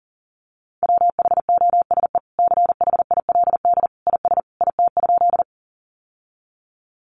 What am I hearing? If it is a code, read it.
Morse code, heard as a keyed tone, about 37 wpm, 700 Hz. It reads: WHOSE CHILD IS IT?
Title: Sentences from Top 1200 Words 40wpm